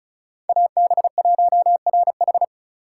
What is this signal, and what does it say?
Morse code, heard as a keyed tone, about 36 wpm, 700 Hz. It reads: AB1RH
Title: USA Call Signs - 3x Character Spacing - 35wpm